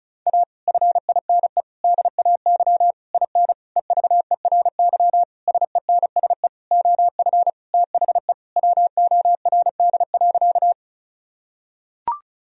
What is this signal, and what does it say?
Morse code, heard as a keyed tone, about 34 wpm, 700 Hz. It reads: A FINE DAY IN EVERY SENSE OF THE WORD.
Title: Sentences from Top 1000 Words - Review Part 4 - 35wpm